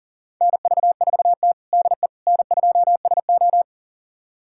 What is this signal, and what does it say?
Morse code, heard as a keyed tone, about 40 wpm, 690 Hz. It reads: NV4T DE N2SO
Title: USA Callsign DE Callsign - 4 Characters - 40wpm